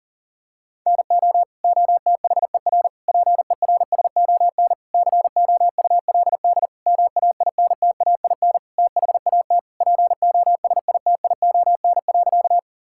Morse code, heard as a keyed tone, about 40 wpm, 700 Hz. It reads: NO OTHER PERSON COULD MAINTAIN THAT POSITION.
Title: Sentences from Top 1000 Words - Review Part 5 - 40wpm